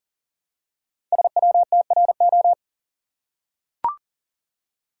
Morse code, heard as a keyed tone, about 39 wpm, 700 Hz. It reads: SWTRO
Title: Code Groups - 5 Letters - 40wpm